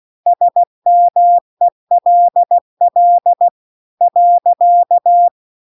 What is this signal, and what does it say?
Morse code, heard as a keyed tone, about 16 wpm, 700 Hz. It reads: SMELL .